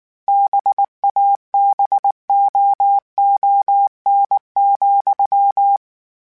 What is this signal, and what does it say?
Morse code, heard as a keyed tone, about 19 wpm, 800 Hz. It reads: BABOON,